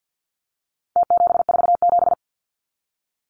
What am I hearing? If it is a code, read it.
Morse code, heard as a keyed tone, about 50 wpm, 700 Hz. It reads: T747
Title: Numbers - Contest Style - 4-Digit 50wpm